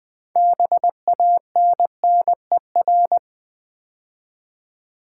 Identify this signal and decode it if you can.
Morse code, heard as a keyed tone, about 20 wpm, 700 Hz. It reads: BANNER
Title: Build A Word - 2 Letters At A Time - 6-Letter Words - 20wpm